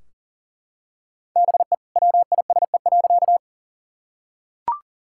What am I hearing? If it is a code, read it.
Morse code, heard as a keyed tone, about 40 wpm, 700 Hz. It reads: BE WISE.
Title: Sentences from Top 1600 Words 40wpm